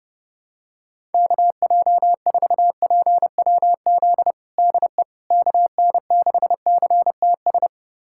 Morse code, heard as a keyed tone, about 30 wpm, 700 Hz. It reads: KJ4PWZ DE KN6CTS
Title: USA Callsign DE Callsign 30wpm